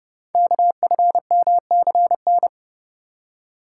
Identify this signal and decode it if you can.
Morse code, heard as a keyed tone, about 30 wpm, 690 Hz. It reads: KFMCN
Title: Code Groups - 5 Letters - 30wpm